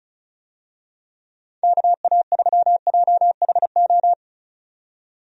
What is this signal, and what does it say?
Morse code, heard as a keyed tone, about 34 wpm, 700 Hz. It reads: KA3JHO
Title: USA Call Signs - 6 Characters - 35wpm